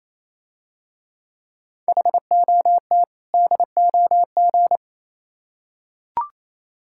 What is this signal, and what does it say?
Morse code, heard as a keyed tone, about 27 wpm, 700 Hz. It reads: HOT DOG
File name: Easy Words - Speed-Racing with Hints - 22wpm